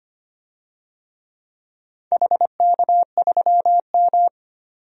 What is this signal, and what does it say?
Morse code, heard as a keyed tone, about 25 wpm, 700 Hz. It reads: HK3M
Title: DXCC-100 Call Signs - 4 Characters - 25wpm